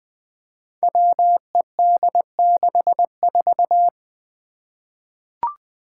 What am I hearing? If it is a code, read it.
Morse code, heard as a keyed tone, about 20 wpm, 700 Hz. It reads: WED64
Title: Five Challenging Letter-Number Combinations 20wpm